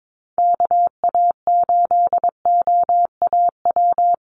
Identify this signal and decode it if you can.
Morse code, heard as a keyed tone, about 22 wpm, 700 Hz. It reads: KA8OAW
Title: USA Call Signs Repeated 3x 22wpm